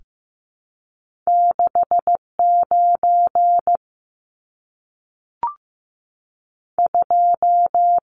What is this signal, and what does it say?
Morse code, heard as a keyed tone, about 15 wpm, spaced as for 3 wpm, 700 Hz. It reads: 6 92
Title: Numbers - 2-Digit 15wpm